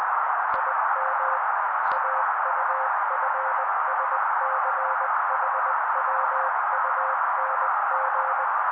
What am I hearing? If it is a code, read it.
Morse code, heard as a keyed tone, about 20 wpm, 530 Hz. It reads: IM AUFSCHWUNG
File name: daily / 18wpm noise